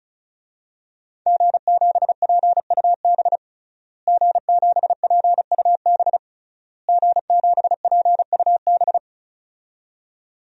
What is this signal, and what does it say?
Morse code, heard as a keyed tone, about 34 wpm, 700 Hz. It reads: G7PUB G7PUB G7PUB
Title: DXCC-100 Call Signs - 5 Characters - Repeated 3x - 35wpm